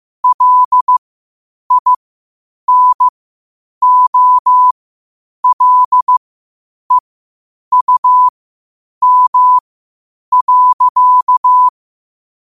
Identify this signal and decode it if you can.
Morse code, heard as a keyed tone, about 15 wpm, spaced as for 8 wpm, 1000 Hz. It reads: LINOLEUM.
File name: 8wpm